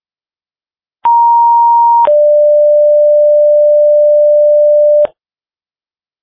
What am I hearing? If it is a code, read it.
Two-tone page, heard as a one-second tone, then a three-second tone, about 940 then 590 Hz. It is heard from 1.0 to 5.1 s.
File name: dispatch / Tripoli Fire 2026 09 03 19 08 25